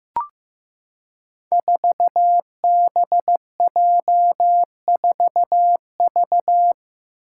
Morse code, heard as a keyed tone, about 15 wpm, 700 Hz. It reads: 4BJ4V
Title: Five Challenging Letter-Number Combinations 15wpm